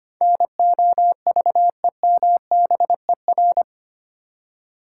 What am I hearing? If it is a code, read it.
Morse code, heard as a keyed tone, about 25 wpm, 700 Hz. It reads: NOVEMBER
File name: Eight Letter Words - 25wpm